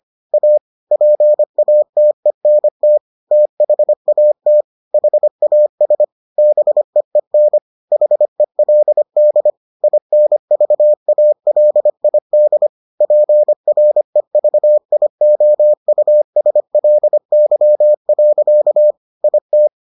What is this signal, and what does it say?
Morse code, heard as a keyed tone, about 25 wpm, 600 Hz. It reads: A PATENT THAT HAS BEEN HELD INVALID PREVIOUSLY. IT